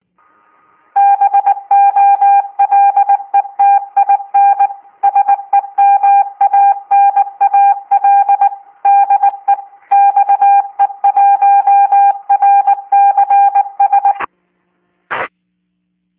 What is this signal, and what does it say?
Morse code, heard as a keyed tone, about 19 wpm, 800 Hz. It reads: BOLETIN SEMANAL DE XE1RCS